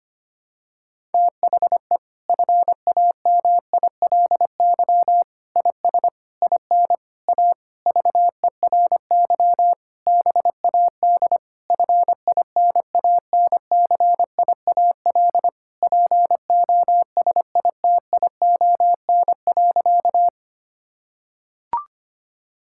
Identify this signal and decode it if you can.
Morse code, heard as a keyed tone, about 25 wpm, 700 Hz. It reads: THE FAMILY IS IN A VERY BAD FINANCIAL POSITION.
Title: Sentences from Top 1000 Words - Review Part 5 - 25wpm